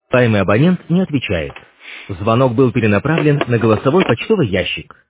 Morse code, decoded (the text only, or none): none